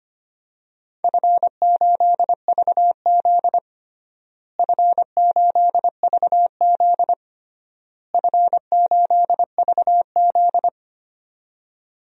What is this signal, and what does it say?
Morse code, heard as a keyed tone, about 25 wpm, 700 Hz. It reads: F8VZ F8VZ F8VZ
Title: DXCC-100 Call Signs - 4 Characters - Repeated 3x - 25wpm